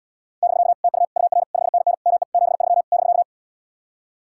Morse code, heard as a keyed tone, about 80 wpm, 700 Hz. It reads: <BK> TU UR 5NN ME 73 <BK>